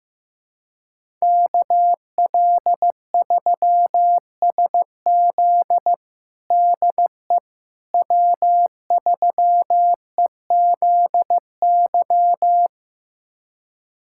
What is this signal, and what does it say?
Morse code, heard as a keyed tone, about 16 wpm, 700 Hz. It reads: KL3SZ DE W3EZY
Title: USA Callsign DE Callsign - 5 Characters - 15wpm